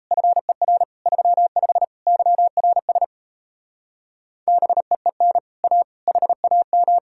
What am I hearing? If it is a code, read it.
Morse code, heard as a keyed tone, about 37 wpm, 700 Hz. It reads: FER 35 YRS BEEN A HAM